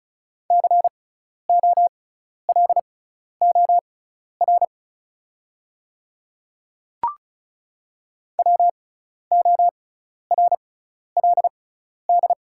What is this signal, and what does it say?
Morse code, heard as a keyed tone, about 34 wpm, spaced as for 12 wpm, 690 Hz. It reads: COLOR WORLD
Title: 6x Character Spacing - Top 300 Words - 35wpm